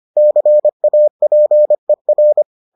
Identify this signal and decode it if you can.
Morse code, heard as a keyed tone, about 25 wpm, 600 Hz. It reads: CAPER